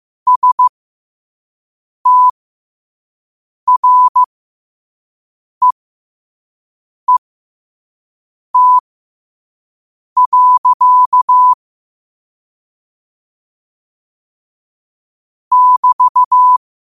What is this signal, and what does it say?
Morse code, heard as a keyed tone, about 15 wpm, spaced as for 5 wpm, 1000 Hz. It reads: STREET. <BT>
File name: qotdfiles / 5wpm